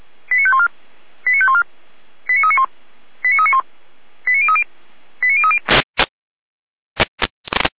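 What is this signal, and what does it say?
Five-tone selective call (ZVEI), heard as quick runs of five tones, tones of about 70 ms. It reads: 89514 89514 89391 89391 89030 89030
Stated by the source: ZVEI five-tone call 89502 received by SDR.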